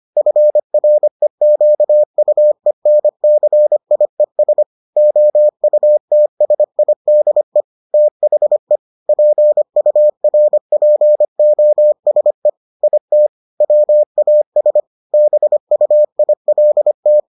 Morse code, heard as a keyed tone, about 25 wpm, 600 Hz. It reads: FREQUENCIES OUTSIDE THE PURPOSE IT WAS BUILT